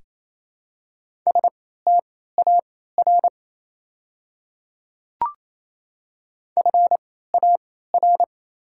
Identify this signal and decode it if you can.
Morse code, heard as a keyed tone, about 28 wpm, spaced as for 16 wpm, 700 Hz. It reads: STAR FAR
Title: 3x Character Spacing - Top 500 Words - 28wpm